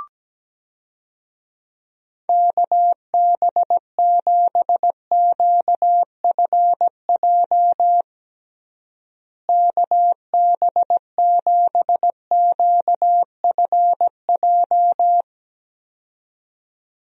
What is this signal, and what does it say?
Morse code, heard as a keyed tone, about 17 wpm, 700 Hz. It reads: KB7QFJ KB7QFJ
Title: USA Call Signs - 6 Characters - Repeated 2x - 17wpm